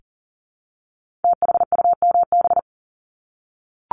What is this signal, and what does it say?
Morse code, heard as a keyed tone, about 38 wpm, 700 Hz. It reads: THUMB
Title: Multi-letter Phonemes Repeated 3x 40wpm